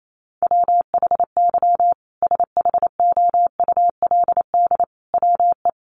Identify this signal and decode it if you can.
Morse code, heard as a keyed tone, about 28 wpm, 700 Hz. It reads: WHY SHOULD WE
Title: Sentences from Top 1500 Words 28wpm